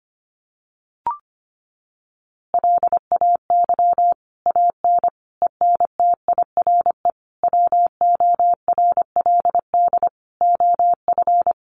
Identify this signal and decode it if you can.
Morse code, heard as a keyed tone, about 25 wpm, 700 Hz. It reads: LAY AN ENTIRE WORLD OF
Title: A Sound Of Thunder 25wpm